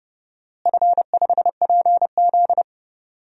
Morse code, heard as a keyed tone, about 30 wpm, 700 Hz. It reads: F5PZ
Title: DXCC-100 Call Signs - 4 Characters - Repeated 3x - 30wpm